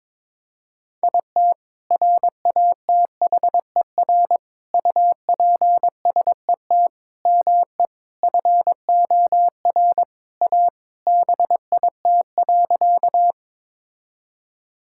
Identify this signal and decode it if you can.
Morse code, heard as a keyed tone, about 22 wpm, 700 Hz. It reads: IT RATHER UPSET ME FOR A BIT.
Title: Sentences from Top 1000 Words - Review Part 5 - 22wpm